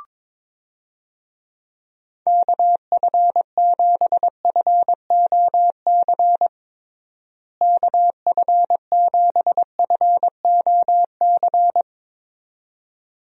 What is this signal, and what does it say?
Morse code, heard as a keyed tone, about 22 wpm, 700 Hz. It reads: KF7FOC KF7FOC
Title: USA Call Signs - 6 Characters - Repeated 2x - 22wpm